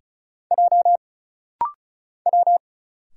Morse code, heard as a keyed tone, about 36 wpm, 700 Hz. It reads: J W
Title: Single Letter-Number Alt - Rapid-Fire 35wpm